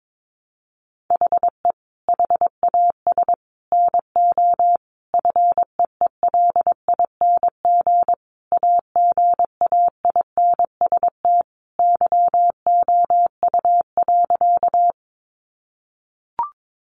Morse code, heard as a keyed tone, about 22 wpm, 700 Hz. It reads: HE HAS NO FEELING AGAINST YOU.
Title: Sentences from Top 1200 Words 22wpm